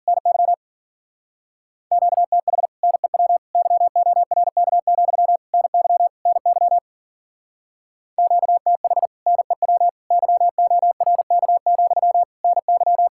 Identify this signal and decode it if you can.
Morse code, heard as a keyed tone, about 50 wpm, 700 Hz. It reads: NY QTH NEW YORK, NY NY QTH NEW YORK, NY